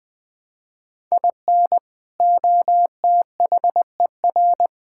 Morse code, heard as a keyed tone, about 21 wpm, 700 Hz. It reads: IN OTHER
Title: Build A Phrase - 3-Word Phrases - 20wpm